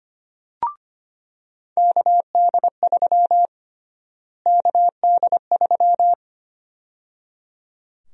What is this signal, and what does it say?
Morse code, heard as a keyed tone, about 25 wpm, 700 Hz. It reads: KD3 KD3
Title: USA Call Sign Prefixes - Repeated 2x - 25wpm